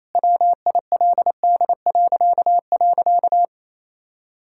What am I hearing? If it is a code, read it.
Morse code, heard as a keyed tone, about 28 wpm, 700 Hz. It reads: WILD..
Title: Pangram Challenge v2 - 28wpm